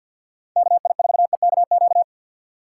Morse code, heard as a keyed tone, about 49 wpm, 700 Hz. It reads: KI4EKQ
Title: USA Call Signs - 6 Characters - 6x Character Spacing - 50wpm